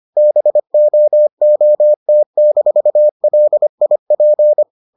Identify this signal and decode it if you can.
Morse code, heard as a keyed tone, about 25 wpm, 600 Hz. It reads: BOOT-LIP